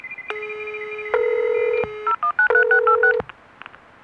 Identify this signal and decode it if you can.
DTMF, heard as telephone touch tones, tones of about 100 ms. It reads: 0193686
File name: Modem Music